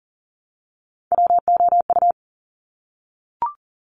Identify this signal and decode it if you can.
Morse code, heard as a keyed tone, about 39 wpm, 700 Hz. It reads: WOU
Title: Top Three Letter Combinations in Words 40wpm